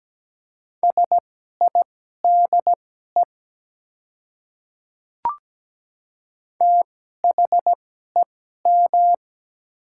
Morse code, heard as a keyed tone, about 17 wpm, spaced as for 12 wpm, 700 Hz. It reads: SIDE THEM
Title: 2x Character Spacing - Top 100 Words - 17wpm